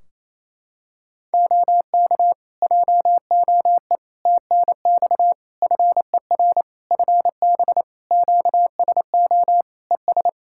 Morse code, heard as a keyed tone, about 28 wpm, 700 Hz. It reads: OK JOE TNX FER FB QSO ES
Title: Practice QSO - Exchange 4 - Ending - 28wpm